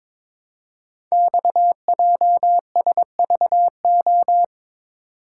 Morse code, heard as a keyed tone, about 22 wpm, 700 Hz. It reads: XJSVO
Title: Code Groups - 5 Letters - 22wpm